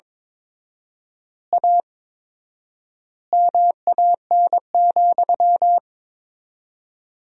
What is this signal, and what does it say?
Morse code, heard as a keyed tone, about 23 wpm, 700 Hz. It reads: A MAN,
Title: 4x Word-Spacing - Sentences from Top 200 Words - 22wpm